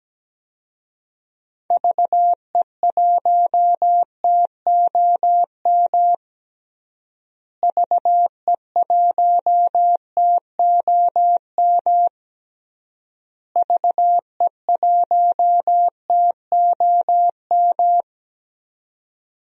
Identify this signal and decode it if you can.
Morse code, heard as a keyed tone, about 17 wpm, 700 Hz. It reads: VE1TOM VE1TOM VE1TOM
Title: DXCC-100 Call Signs - 6 Characters - Repeated 3x - 17wpm